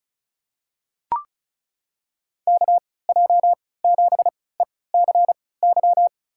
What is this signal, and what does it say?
Morse code, heard as a keyed tone, about 36 wpm, spaced as for 20 wpm, 700 Hz. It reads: KJ7ECY
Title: USA Call Signs - 6 Characters - 3x Character Spacing - 35wpm